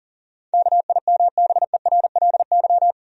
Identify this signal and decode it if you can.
Morse code, heard as a keyed tone, about 37 wpm, 700 Hz. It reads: KIMBERLY
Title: USA Common Names - 40wpm